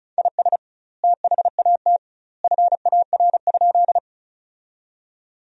Morse code, heard as a keyed tone, about 34 wpm, 700 Hz. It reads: IS THAT FAR?
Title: 2x Word-Spacing - Sentences from Top 300 Words - 35wpm